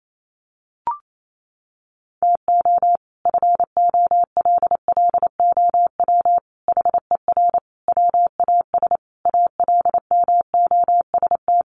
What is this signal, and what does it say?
Morse code, heard as a keyed tone, about 28 wpm, 700 Hz. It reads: TO FOLLOW HER WAS ALMOST
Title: Sentences from Top 1100 Words 28wpm